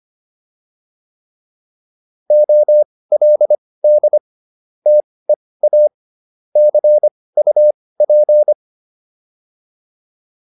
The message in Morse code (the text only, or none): OLD TEA CUP